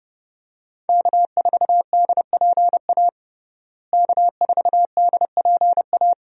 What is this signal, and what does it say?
Morse code, heard as a keyed tone, about 30 wpm, 700 Hz. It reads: K4DPA K4DPA
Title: USA Call Signs - 5 Characters - Repeated 2x - 30wpm